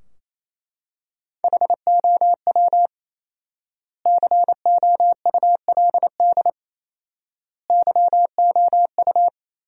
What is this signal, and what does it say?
Morse code, heard as a keyed tone, about 28 wpm, 700 Hz. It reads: HOW COULD YOU
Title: 4x Word-Spacing - Sentences from Top 300 Words - 28wpm